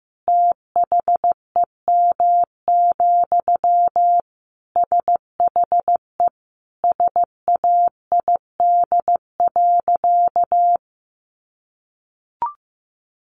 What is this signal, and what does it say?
Morse code, heard as a keyed tone, about 15 wpm, 700 Hz. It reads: THEM, SHE SAID.
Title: Sentences from Top 1400 Words 15wpm